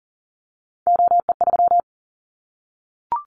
Morse code, heard as a keyed tone, about 40 wpm, 700 Hz. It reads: OE3
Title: Callsign Prefixes for Top 25 DXCC - 40wpm